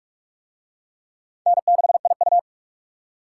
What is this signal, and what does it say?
Morse code, heard as a keyed tone, about 44 wpm, 700 Hz. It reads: N6IU